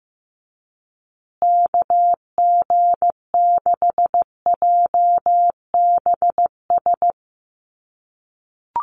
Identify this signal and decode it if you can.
Morse code, heard as a keyed tone, about 15 wpm, 700 Hz. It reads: KG6JBS
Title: USA Call Signs Repeated 3x 15wpm